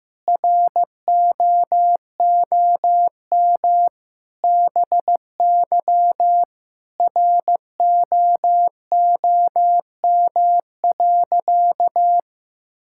Morse code, heard as a keyed tone, about 15 wpm, 700 Hz. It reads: ROOM BY ROOM.
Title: Sentences from Top 1000 Words - Review Part 5 - 15wpm